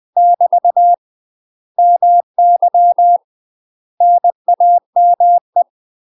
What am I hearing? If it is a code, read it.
Morse code, heard as a keyed tone, about 20 wpm, 700 Hz. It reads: <BT> MY NAME
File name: UK MASTER QSOs TRAINING / 20WPM / UK MASTER QSOs 0008